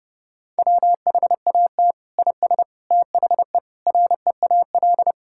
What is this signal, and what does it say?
Morse code, heard as a keyed tone, about 30 wpm, 700 Hz. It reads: WHAT IS THE REAL
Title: Sentences from Top 1000 Words - Review Part 6 - 30wpm